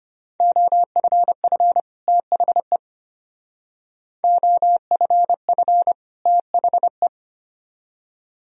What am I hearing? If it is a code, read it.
Morse code, heard as a keyed tone, about 33 wpm, 700 Hz. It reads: OFF THE OFF THE